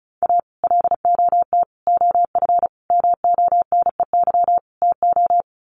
Morse code, heard as a keyed tone, about 34 wpm, 700 Hz. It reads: A LOT OF MONEY TO
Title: Sets of 5 Words Encore2 35wpm